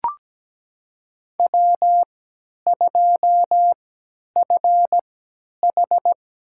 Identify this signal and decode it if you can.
Morse code, heard as a keyed tone, about 17 wpm, spaced as for 10 wpm, 700 Hz. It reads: W2FH
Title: USA Call Signs - 3x Character Spacing - 17wpm